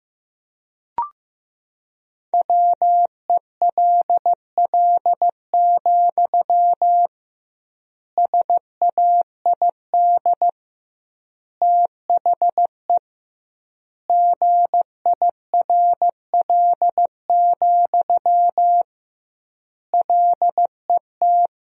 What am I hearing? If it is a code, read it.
Morse code, heard as a keyed tone, about 15 wpm, 700 Hz. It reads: WELL, SAID THE GIRL, LET
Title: 2x Word-Spacing - Sentences from Top 400 Words - 15wpm